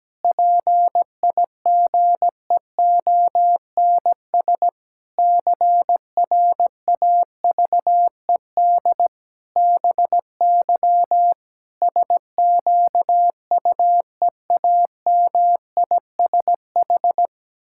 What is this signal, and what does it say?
Morse code, heard as a keyed tone, about 17 wpm, 700 Hz. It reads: PIGEONS CRAVED BY SQUEAMISH
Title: Pangram Challenge v2 - 17wpm